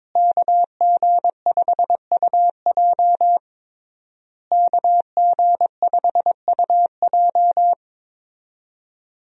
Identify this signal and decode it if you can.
Morse code, heard as a keyed tone, about 22 wpm, 700 Hz. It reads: KG5UJ KG5UJ
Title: USA Call Signs - 5 Characters - Repeated 3x - 22wpm